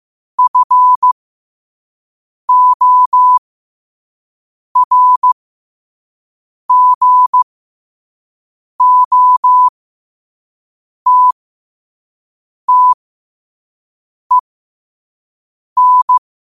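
Morse code, heard as a keyed tone, about 15 wpm, spaced as for 5 wpm, 1000 Hz. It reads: FORGOTTEN
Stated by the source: Morse code sent at 5 wpm.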